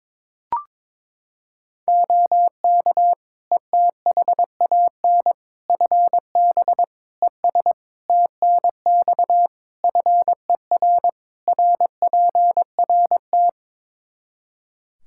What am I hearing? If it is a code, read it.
Morse code, heard as a keyed tone, about 22 wpm, 700 Hz. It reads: OK ETHAN FB ES TNX FER RPRT
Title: Practice QSO - Exchange 2 - Info - 22wpm